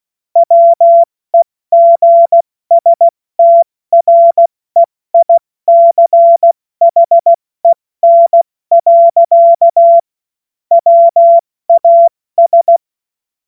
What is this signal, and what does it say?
Morse code, heard as a keyed tone, about 16 wpm, spaced as for 14 wpm, 680 Hz. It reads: WEGSTREICHEN. WAS
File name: Morsekurs / Witze / Witz 21. Juni 2022 70 F80